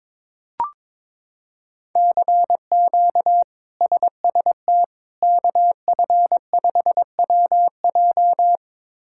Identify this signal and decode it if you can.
Morse code, heard as a keyed tone, about 22 wpm, 700 Hz. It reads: CQ SST KF5WJ
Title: K1USN SST QSO Practice - 22wpm